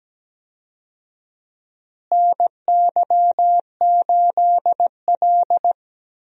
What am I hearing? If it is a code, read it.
Morse code, heard as a keyed tone, about 17 wpm, 700 Hz. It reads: NY8L
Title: USA Call Signs - 4 Characters - Repeated 2x - 17wpm